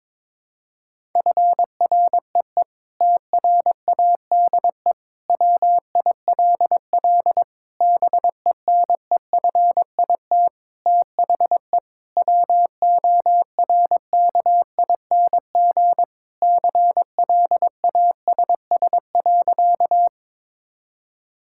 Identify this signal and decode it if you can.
Morse code, heard as a keyed tone, about 22 wpm, 700 Hz. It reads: FREE TRADE WILL BENEFIT THE WORKING CLASS.
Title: Sentences from Top 1000 Words - Review Part 6 - 22wpm